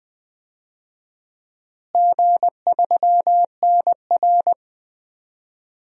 Morse code, heard as a keyed tone, about 21 wpm, 700 Hz. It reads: G3NR
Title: DXCC-100 Call Signs - 4 Characters - Repeated 3x - 20wpm